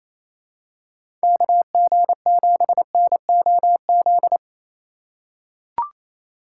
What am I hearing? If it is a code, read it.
Morse code, heard as a keyed tone, about 28 wpm, 700 Hz. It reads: KG7NOZ
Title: USA Call Signs - 6 Characters - 4x Character Spacing - 28wpm